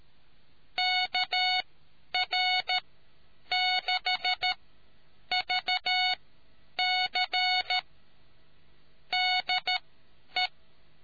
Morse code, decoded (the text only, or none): KR6VC DE